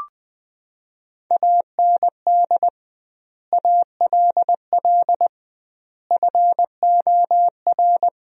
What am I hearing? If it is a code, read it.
Morse code, heard as a keyed tone, about 20 wpm, 700 Hz. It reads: AND ALL FOR
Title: 2x Word-Spacing - Sentences from Top 100 Words - 20wpm